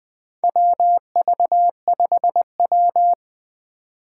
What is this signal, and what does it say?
Morse code, heard as a keyed tone, about 20 wpm, 700 Hz. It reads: WV5W
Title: USA Call Signs - 3x Character Spacing - 20wpm